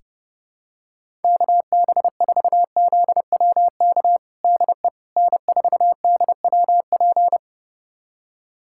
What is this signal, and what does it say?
Morse code, heard as a keyed tone, about 30 wpm, 700 Hz. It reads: KB4ZWK DE N4DWP
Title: USA Callsign DE Callsign 30wpm